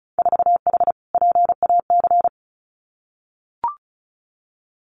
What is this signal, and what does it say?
Morse code, heard as a keyed tone, about 36 wpm, 700 Hz. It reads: 4H PAC
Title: Winter Field Day Exchanges Encore 35wpm